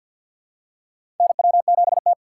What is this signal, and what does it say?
Morse code, heard as a keyed tone, about 53 wpm, 700 Hz. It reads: NW7T
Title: USA Call Signs - 2x Character Spacing - 50wpm